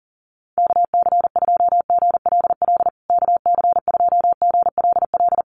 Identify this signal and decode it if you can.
Morse code, heard as a keyed tone, about 40 wpm, 700 Hz. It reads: KC2GLL KC2GLL
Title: USA Call Signs Repeated 3x 40wpm